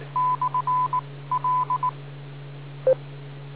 Morse code, heard as a keyed tone, about 19 wpm, 1000 Hz. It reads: / L